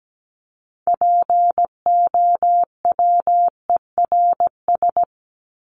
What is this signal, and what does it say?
Morse code, heard as a keyed tone, about 17 wpm, 700 Hz. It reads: POWERS
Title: ES:S Suffix - Top 100 Words with Base Words 17wpm